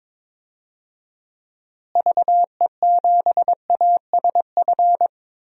Morse code, heard as a keyed tone, about 22 wpm, 700 Hz. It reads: VE7ASF
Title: DXCC-100 Call Signs - 6 Characters - Repeated 3x - 22wpm